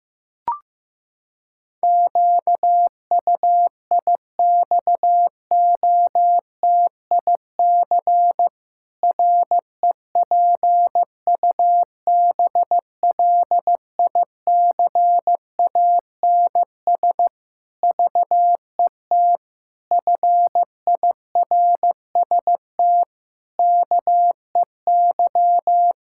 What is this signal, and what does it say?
Morse code, heard as a keyed tone, about 15 wpm, 700 Hz. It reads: QUIXOTIC REPUBLICANS VET FIRST KEY
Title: Pangram Challenge v2 - 15wpm